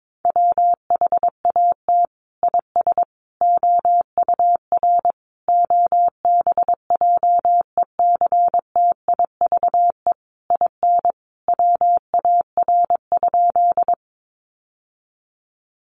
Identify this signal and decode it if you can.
Morse code, heard as a keyed tone, about 22 wpm, 700 Hz. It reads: WHAT IS OUR OBJECTIVE IN WAR?
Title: Sentences from Top 1200 Words 22wpm